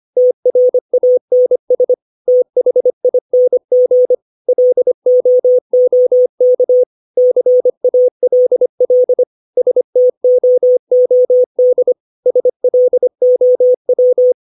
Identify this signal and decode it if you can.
Morse code, heard as a keyed tone, about 25 wpm, 500 Hz. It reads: TRANS THING LOOK CALL STOOD SLOW
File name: 632 words / 632 Words 25WPM